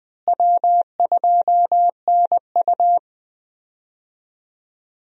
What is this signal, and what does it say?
Morse code, heard as a keyed tone, about 20 wpm, 700 Hz. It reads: W2NU
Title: USA Call Signs - 4 Characters - Repeated 3x - 20wpm